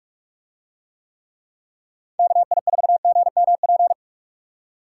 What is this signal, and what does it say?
Morse code, heard as a keyed tone, about 47 wpm, 690 Hz. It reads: KI4GMP